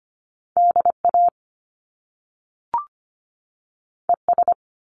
Morse code, heard as a keyed tone, about 25 wpm, 700 Hz. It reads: DA ES
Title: Top Two Letter Combinations in Words 25wpm